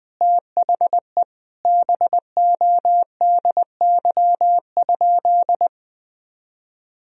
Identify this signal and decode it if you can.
Morse code, heard as a keyed tone, about 19 wpm, 700 Hz. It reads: THE BODY?
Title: Sentences from Top 400 Words and Abbreviations 20wpm